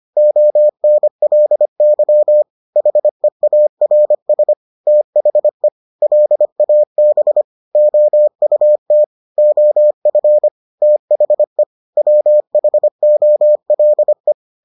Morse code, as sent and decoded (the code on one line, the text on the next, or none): --- -. .-.. -.-- / .... .- .-. ... / - .... / .-.. .- -... / --- ..- - / --- ..-. / - .... / .-- .... --- .-.. .
ONLY HEARS THE LAB OUT OF THE WHOLE